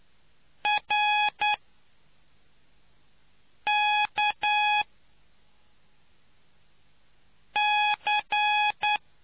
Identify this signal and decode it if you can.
Morse code, heard as a keyed tone, about 10 wpm, spaced as for 3 wpm, 840 Hz. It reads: RKC